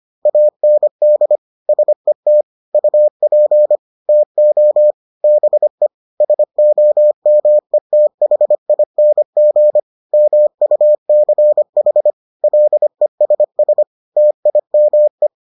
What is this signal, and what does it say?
Morse code, heard as a keyed tone, about 25 wpm, 600 Hz. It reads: AND SET UP TO BE SOMETHING MUCH LESS TIME